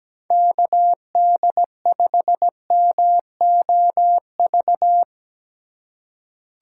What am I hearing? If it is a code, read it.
Morse code, heard as a keyed tone, about 17 wpm, 700 Hz. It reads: KD5MOV
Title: USA Call Signs - 6 Characters - 3x Character Spacing - 17wpm